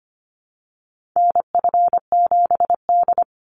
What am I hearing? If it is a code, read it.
Morse code, heard as a keyed tone, about 25 wpm, 700 Hz. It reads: NF7D
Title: Call Signs - 25wpm